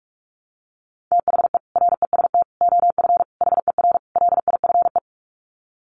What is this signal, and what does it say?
Morse code, heard as a keyed tone, about 44 wpm, 700 Hz. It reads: THE REST OF HER LIFE
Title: Sets of 5 Words Encore 45wpm